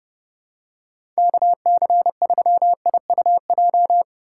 Morse code, heard as a keyed tone, about 32 wpm, 700 Hz. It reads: KC3IUJ
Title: USA Call Signs - 6 Characters - 3x Character Spacing - 30wpm